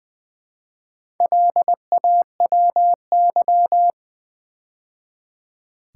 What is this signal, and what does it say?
Morse code, heard as a keyed tone, about 20 wpm, 700 Hz. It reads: LAWY
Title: Build A Word - 2 Letters At A Time - 6-Letter Words - 20wpm